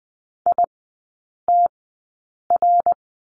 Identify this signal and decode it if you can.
Morse code, heard as a keyed tone, about 20 wpm, spaced as for 8 wpm, 700 Hz. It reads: ITR
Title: 053 Lesson 10e - Letter r - Three Characters 20wpm